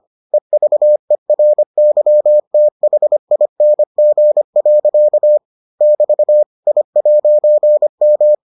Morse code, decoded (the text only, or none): EVERYTHING. <BT> I'M